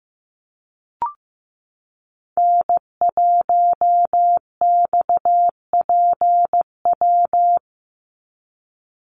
Morse code, heard as a keyed tone, about 15 wpm, 700 Hz. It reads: N1XPW
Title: USA Call Signs 15wpm